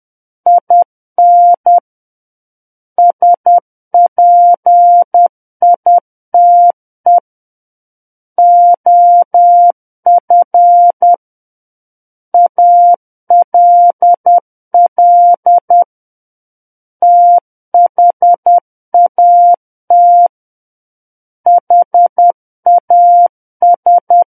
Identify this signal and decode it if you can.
Morse code, heard as a keyed tone, about 10 wpm, 700 Hz. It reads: IN SPITE OF ALL THAT HAS
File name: Book One / 01-Chapter 1 - 10 WPM